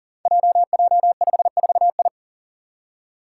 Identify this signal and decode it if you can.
Morse code, heard as a keyed tone, about 39 wpm, 700 Hz. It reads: JJ54I
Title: Five Challenging Letter-Number Combinations 40wpm